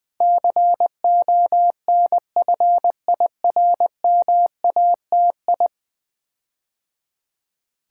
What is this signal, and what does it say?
Morse code, heard as a keyed tone, about 20 wpm, 700 Hz. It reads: CONFIRMATI
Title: Build A Word - 5 Letters At A Time - 12-Letter Words - 20wpm